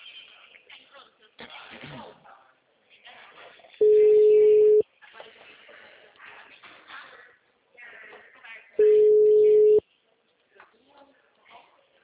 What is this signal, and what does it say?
Ringback tone (Europe, 425 Hz), with speech.